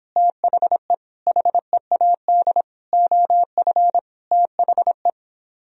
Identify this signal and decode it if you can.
Morse code, heard as a keyed tone, about 26 wpm, 700 Hz. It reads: THE HEAD OF THE